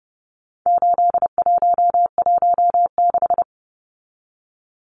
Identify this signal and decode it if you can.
Morse code, heard as a keyed tone, about 30 wpm, 700 Hz. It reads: 8116
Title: Numbers - 4-Digit 30wpm